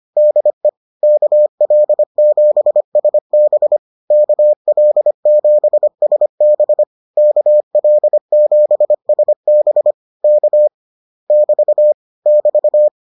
Morse code, heard as a keyed tone, about 25 wpm, 600 Hz. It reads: DE KL7SB KL7SB KL7SB K <BT> <BT>